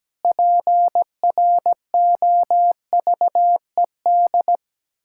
Morse code, heard as a keyed tone, about 17 wpm, 700 Hz. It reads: PROVED
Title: Pangram Challenge v2 - 17wpm